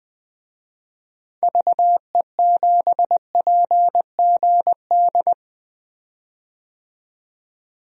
Morse code, heard as a keyed tone, about 20 wpm, 700 Hz. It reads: VE7PGD
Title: DXCC-100 Call Signs - 6 Characters - 20wpm